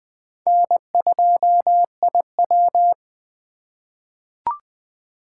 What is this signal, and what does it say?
Morse code, heard as a keyed tone, about 20 wpm, 700 Hz. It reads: N2IW
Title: USA Call Signs - 4 Characters - 2x Character Spacing - 20wpm